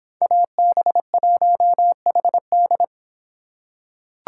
Morse code, heard as a keyed tone, about 25 wpm, 700 Hz. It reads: AB1HD